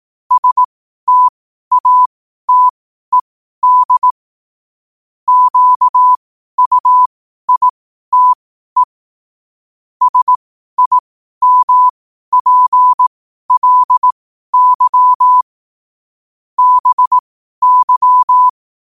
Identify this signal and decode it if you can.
Morse code, heard as a keyed tone, about 18 wpm, spaced as for 12 wpm, 1000 Hz. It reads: STATED QUITE SIMPLY BY